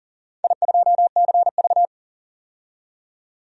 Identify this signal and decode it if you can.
Morse code, heard as a keyed tone, about 40 wpm, 700 Hz. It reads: I2CV
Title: DXCC-100 Call Signs - 4 Characters - 40wpm